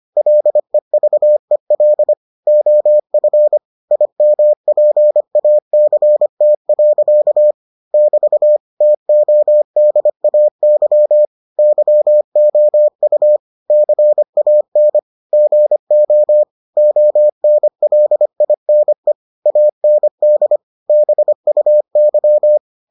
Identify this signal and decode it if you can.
Morse code, heard as a keyed tone, about 25 wpm, 600 Hz. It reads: LEVEL OF IMPACT. <BT> TODAY YOU CAN GO ONLINE AND BUY